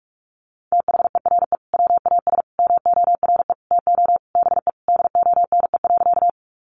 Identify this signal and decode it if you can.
Morse code, heard as a keyed tone, about 44 wpm, 700 Hz. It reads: THERE WAS MORE TO BE DONE.
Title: Sentences from Top 500 Words 45wpm